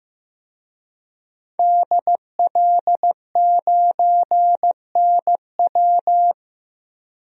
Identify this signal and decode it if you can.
Morse code, heard as a keyed tone, about 15 wpm, 700 Hz. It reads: DL9NW
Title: DXCC-100 Call Signs - 5 Characters - Repeated 2x - 15wpm